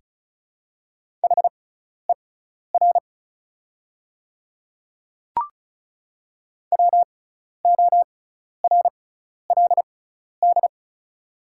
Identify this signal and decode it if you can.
Morse code, heard as a keyed tone, about 36 wpm, spaced as for 12 wpm, 690 Hz. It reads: HER WORLD
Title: 6x Character Spacing - Top 300 Words - 35wpm